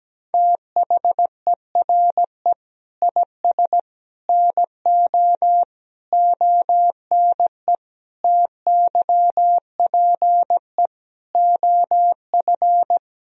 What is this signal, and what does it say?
Morse code, heard as a keyed tone, about 17 wpm, 700 Hz. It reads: THERE IS NO ONE TYPE OF
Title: Sentences from Top 1000 Words - Review Part 3 - 17wpm